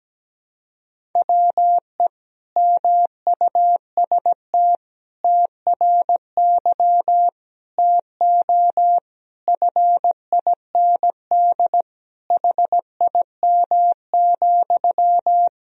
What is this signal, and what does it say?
Morse code, heard as a keyed tone, about 17 wpm, 700 Hz. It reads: WE MUST TRY TO FIND HIM,
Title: Sentences from Top 400 Words and Abbreviations 17wpm